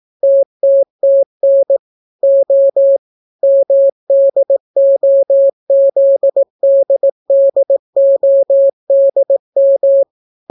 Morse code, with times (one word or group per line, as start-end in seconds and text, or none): 0.2-1.8 TTTN
2.2-3.0 O
3.4-10.0 MDOZDDODM